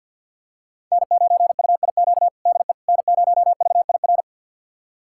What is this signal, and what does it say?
Morse code, heard as a keyed tone, about 52 wpm, 700 Hz. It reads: N9UIQ DE N0VIR
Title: USA Callsign DE Callsign - 5 Characters - 50wpm